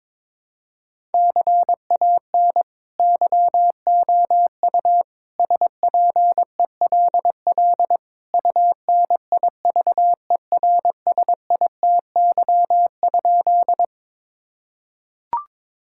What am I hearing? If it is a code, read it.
Morse code, heard as a keyed tone, about 22 wpm, 700 Hz. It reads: CAN YOU SPELL UNIVERSITY?
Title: Sentences from Top 1000 Words - Review Part 5 - 22wpm